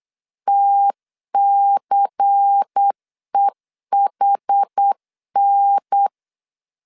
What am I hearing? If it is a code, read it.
Morse code, heard as a keyed tone, about 8 wpm, 800 Hz. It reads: TCEHN
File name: 8wpm